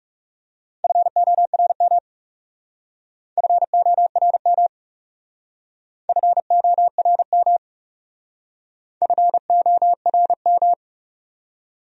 Morse code, heard as a keyed tone, about 37 wpm, 700 Hz. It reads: FORM FORM FORM FORM
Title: Speed-Racing Top 300 Words 30wpm